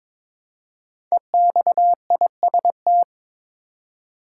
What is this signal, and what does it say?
Morse code, heard as a keyed tone, about 22 wpm, 700 Hz. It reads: EXIST
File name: 5 Letter Words - First 3 Letters Given - 22wpm